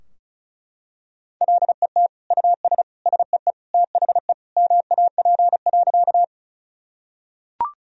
Morse code, heard as a keyed tone, about 36 wpm, 700 Hz. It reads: LET US SEE THE MAP.
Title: Sentences from Top 1000 Words - Review Part 2 - 35wpm